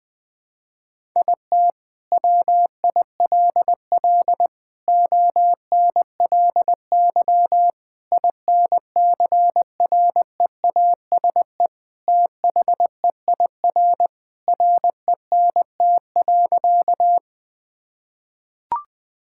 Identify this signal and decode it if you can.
Morse code, heard as a keyed tone, about 21 wpm, 700 Hz. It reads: IT WILL ONLY INCREASE THEIR RENT.
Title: Sentences from Top 1000 Words - Review Part 5 - 20wpm